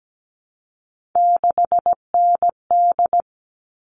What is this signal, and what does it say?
Morse code, heard as a keyed tone, about 17 wpm, 700 Hz. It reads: 6ND
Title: All Cut Numbers - 3-Digit 17wpm